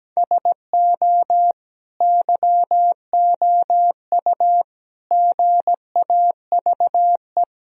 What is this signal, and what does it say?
Morse code, heard as a keyed tone, about 17 wpm, 700 Hz. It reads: SO YOU GAVE
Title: Sentences from Top 1000 Words - Review Part 2 - 17wpm